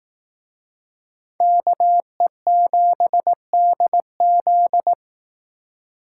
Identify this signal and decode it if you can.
Morse code, heard as a keyed tone, about 18 wpm, 700 Hz. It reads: KE7DZ
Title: Speed Racing - USA 2x2 Call Signs 15wpm